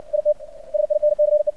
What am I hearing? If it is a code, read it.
Morse code, heard as a keyed tone, about 68 wpm, 600 Hz. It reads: IT SINCE
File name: archived / CRSnet